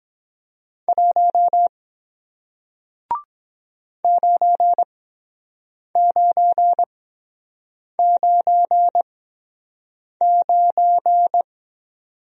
Morse code, heard as a keyed tone, about 23 wpm, spaced as for 7 wpm, 700 Hz. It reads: 1 9999